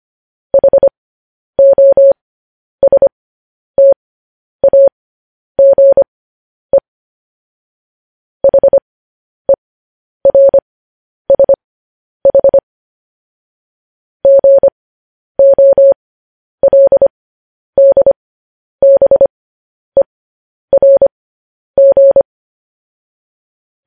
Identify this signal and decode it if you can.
Morse code, heard as a keyed tone, about 25 wpm, spaced as for 10 wpm, 550 Hz. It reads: HOSTAGE HERSH GOLDBERG
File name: podcast / News headlines-1x-25x10